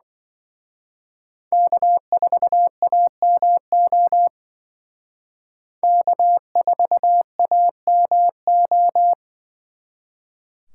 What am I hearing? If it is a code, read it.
Morse code, heard as a keyed tone, about 24 wpm, 700 Hz. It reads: K4AMO K4AMO
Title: Speed Racing - USA 1x3 Call Signs 20wpm